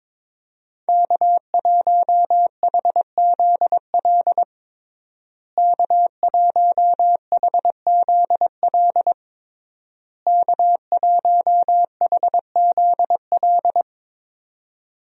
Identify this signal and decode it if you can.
Morse code, heard as a keyed tone, about 22 wpm, 700 Hz. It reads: K1HZL K1HZL K1HZL
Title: USA Call Signs - 5 Characters - Repeated 3x - 22wpm